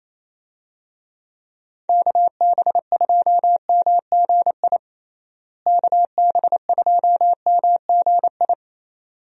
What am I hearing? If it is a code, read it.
Morse code, heard as a keyed tone, about 27 wpm, 700 Hz. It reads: KB2MGI KB2MGI
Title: USA Call Signs - 6 Characters - Repeated 2x - 28wpm